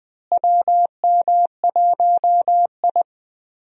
Morse code, heard as a keyed tone, about 21 wpm, 700 Hz. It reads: WM1I
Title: USA Call Signs - 4 Characters - Repeated 3x - 20wpm